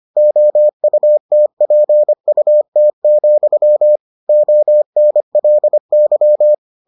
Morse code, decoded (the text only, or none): OUTPUT, ONLY